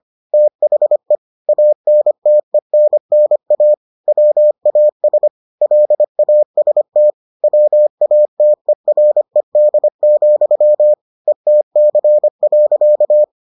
Morse code, heard as a keyed tone, about 25 wpm, 600 Hz. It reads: THE ANTENNA WAS LAST WATERED, ETC.